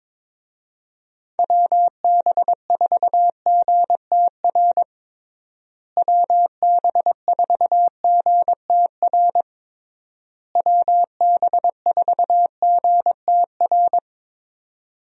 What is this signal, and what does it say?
Morse code, heard as a keyed tone, about 22 wpm, 700 Hz. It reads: WB4GTR WB4GTR WB4GTR
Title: USA Call Signs - 6 Characters - Repeated 3x - 22wpm